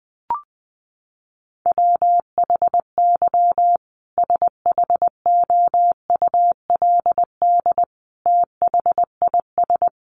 Morse code, heard as a keyed tone, about 20 wpm, 700 Hz. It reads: WHY SHOULD THIS
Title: Sentences from Top 300 Words 20wpm